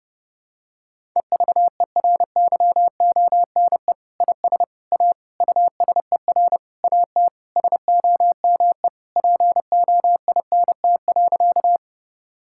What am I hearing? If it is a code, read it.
Morse code, heard as a keyed tone, about 30 wpm, 700 Hz. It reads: EVERYONE IS A USER AT SOME POINT.
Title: Sentences from Top 1000 Words - Review Part 5 - 30wpm